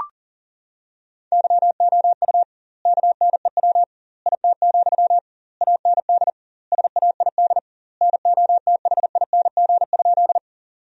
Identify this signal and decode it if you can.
Morse code, heard as a keyed tone, about 40 wpm, 700 Hz. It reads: YOU KNEW IT, AND SAID NOTHING?
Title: 2x Word-Spacing - Sentences from Top 500 Words - 40wpm